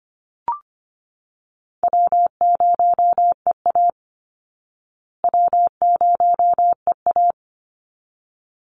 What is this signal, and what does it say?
Morse code, heard as a keyed tone, about 25 wpm, 700 Hz. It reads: W0EA W0EA
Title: Easy Call Signs Repeated 2x - 25wpm